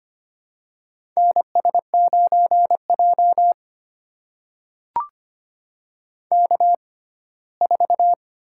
Morse code, heard as a keyed tone, about 25 wpm, spaced as for 17 wpm, 700 Hz. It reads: NS9J K4
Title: USA Call Signs - 6x Character Spacing - 25wpm